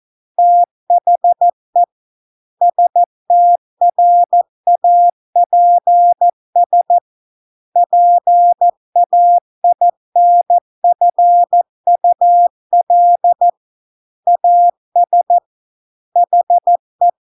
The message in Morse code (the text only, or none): THE STRAPS PAINFUL AS HE